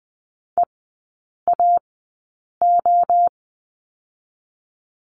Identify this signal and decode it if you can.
Morse code, heard as a keyed tone, about 20 wpm, 700 Hz. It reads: E A O
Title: 029 Lesson 6e - Letter i - Three Characters 20wpm